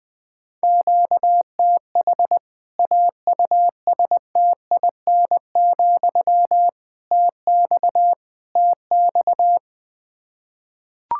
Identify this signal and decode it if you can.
Morse code, heard as a keyed tone, about 20 wpm, 700 Hz. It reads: QTH AUSTIN, TX TX
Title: K1USN SST QSO Practice 20wpm